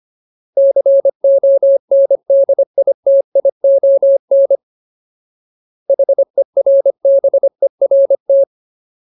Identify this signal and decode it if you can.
Morse code, heard as a keyed tone, about 25 wpm, 550 Hz. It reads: CONDITION HERBERT
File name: Ultimate Word List / 25WPM Ultimate